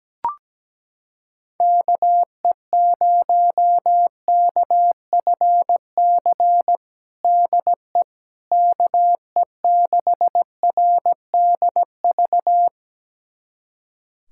Morse code, heard as a keyed tone, about 17 wpm, 700 Hz. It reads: KE0KFC DE KE6RDV
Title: USA Callsign DE Callsign - 6 Characters - 17wpm